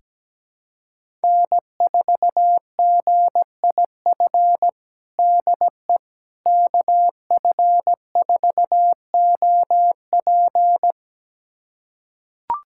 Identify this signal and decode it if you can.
Morse code, heard as a keyed tone, about 17 wpm, 700 Hz. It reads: N4GIF DE KF4OP
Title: USA Callsign DE Callsign - 5 Characters - 17wpm